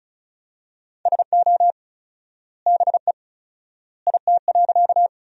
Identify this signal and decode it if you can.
Morse code, heard as a keyed tone, about 36 wpm, 700 Hz. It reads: SO BE IT.
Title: 4x Word-Spacing - Sentences from Top 100 Words - 35wpm